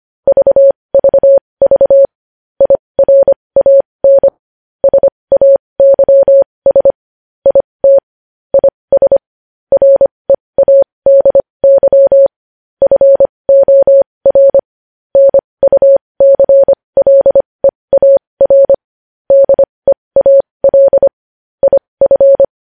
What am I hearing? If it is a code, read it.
Morse code, heard as a keyed tone, about 25 wpm, spaced as for 20 wpm, 550 Hz. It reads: VVV IRAN SAYS IT IS READY FOR NUCLEAR DEAL IF
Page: news headlines in morse code at 20 wpm